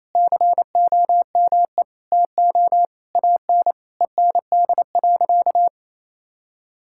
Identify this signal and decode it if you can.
Morse code, heard as a keyed tone, about 27 wpm, 700 Hz. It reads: COME TO AN END.
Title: Sentences from Top 1000 Words - Review Part 6 - 28wpm